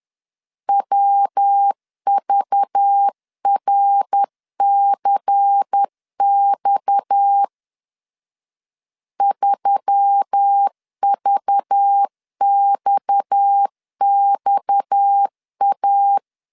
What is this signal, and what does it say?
Morse code, heard as a keyed tone, about 10 wpm, 800 Hz. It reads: WVRCX 3VXXA
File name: LEZIONI / 6 PKX0 / 6 PKX 10wpm 2426